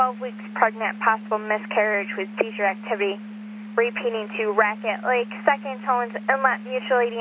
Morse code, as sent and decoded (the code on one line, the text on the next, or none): none